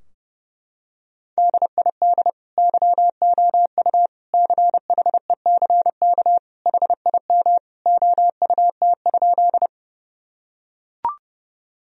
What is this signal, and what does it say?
Morse code, heard as a keyed tone, about 32 wpm, 700 Hz. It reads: DID YOU CHECK HIM OUT?
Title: Sentences from Top 1000 Words - Review Part 3 - 30wpm